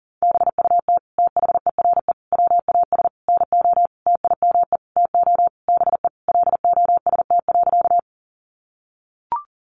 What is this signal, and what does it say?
Morse code, heard as a keyed tone, about 40 wpm, 700 Hz. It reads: BUT THERE WAS NO TIME TO BE LOST.
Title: Sentences from Top 1400 Words 40wpm